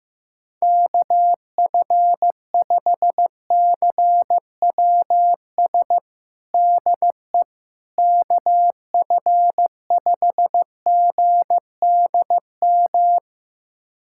Morse code, heard as a keyed tone, about 15 wpm, 700 Hz. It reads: KF5CWS DE KF5GDM
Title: USA Callsign DE Callsign - 6 Characters - 15wpm